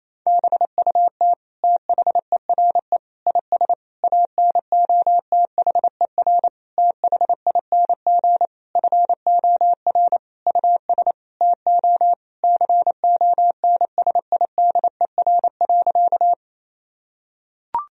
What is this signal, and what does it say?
Morse code, heard as a keyed tone, about 27 wpm, 700 Hz. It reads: BUT THERE IS ANOTHER THING FOR US TO CONSIDER.
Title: Sentences from Top 1000 Words - Review Part 4 - 28wpm